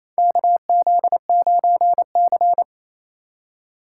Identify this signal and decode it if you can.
Morse code, heard as a keyed tone, about 27 wpm, 700 Hz. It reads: KZ9C
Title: USA Call Signs - 4 Characters - 4x Character Spacing - 28wpm